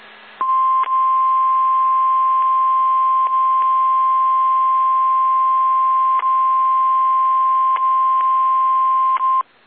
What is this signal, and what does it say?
Emergency Alert System (SAME) tones (the 1050 Hz attention tone). It is heard from 0.4 to 9.5 s.
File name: eas / KSDZKDJL / KSDZKDJL